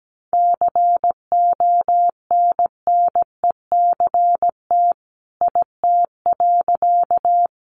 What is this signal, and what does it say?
Morse code, heard as a keyed tone, about 17 wpm, 700 Hz. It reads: CONNECT IT.
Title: Sentences from Top 1500 Words 17wpm